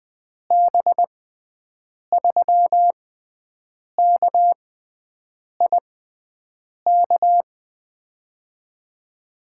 Morse code, heard as a keyed tone, about 20 wpm, spaced as for 7 wpm, 700 Hz. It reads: B3KIK
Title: USA Call Signs - 6 Characters - 6x Character Spacing - 20wpm